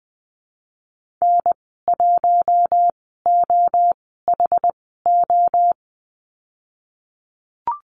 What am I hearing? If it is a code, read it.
Morse code, heard as a keyed tone, about 20 wpm, spaced as for 15 wpm, 700 Hz. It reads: N1OHO
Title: 076 Lesson 13g - Letter L - Call Signs 20wpm